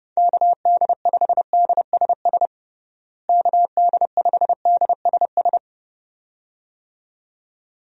Morse code, heard as a keyed tone, about 32 wpm, 700 Hz. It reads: KD5DSS KD5DSS
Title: USA Call Signs - 6 Characters - Repeated 2x - 30wpm